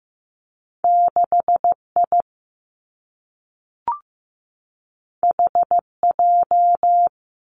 Morse code, heard as a keyed tone, about 15 wpm, 700 Hz. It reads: 6 I H J
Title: Two Challenging Letter-Number Combinations 15wpm